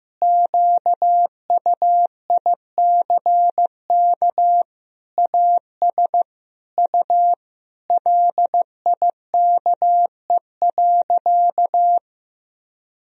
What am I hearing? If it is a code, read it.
Morse code, heard as a keyed tone, about 15 wpm, 700 Hz. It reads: QUICK AS U LIKE.
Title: Sentences from Top 500 Words and Abbreviations 15wpm